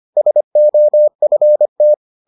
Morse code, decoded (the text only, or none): SOFT